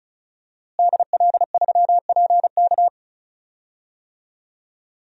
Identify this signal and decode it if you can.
Morse code, heard as a keyed tone, about 34 wpm, 700 Hz. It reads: DL3PK
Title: DXCC-100 Call Signs - 5 Characters - Repeated 2x - 35wpm